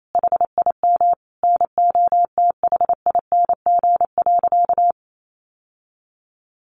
Morse code, heard as a keyed tone, about 28 wpm, 700 Hz. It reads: HIM NOTHING.
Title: Sentences from Top 700 Words 28wpm